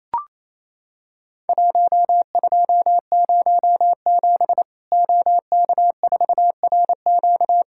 Morse code, heard as a keyed tone, about 28 wpm, 700 Hz. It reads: 1207 OK4RQ
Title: Worked All Europe DX Contest - QTC Exchange - 28wpm